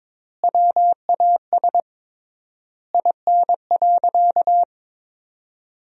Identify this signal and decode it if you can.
Morse code, heard as a keyed tone, about 22 wpm, 700 Hz. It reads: WAS IN.
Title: 3x Word-Spacing - Sentences from Top 400 Words - 22wpm